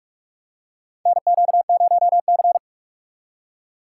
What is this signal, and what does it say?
Morse code, heard as a keyed tone, about 47 wpm, 700 Hz. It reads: NQ0C
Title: USA Call Signs - 2x Character Spacing - 45wpm